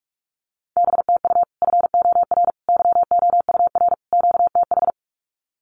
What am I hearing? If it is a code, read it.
Morse code, heard as a keyed tone, about 45 wpm, 700 Hz. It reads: B TU FOR YOUR QTH
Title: Common QSO Phrases 45wpm